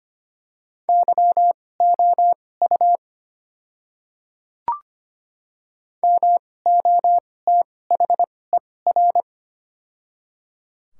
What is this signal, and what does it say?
Morse code, heard as a keyed tone, about 25 wpm, spaced as for 18 wpm, 700 Hz. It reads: YOU MOTHER
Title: 2x Character Spacing - Top 300 Words - 25wpm